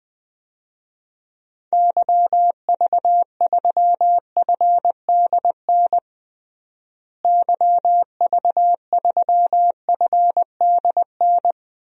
Morse code, heard as a keyed tone, about 20 wpm, 700 Hz. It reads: YV3FDN YV3FDN
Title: DXCC-100 Call Signs - 6 Characters - Repeated 3x - 20wpm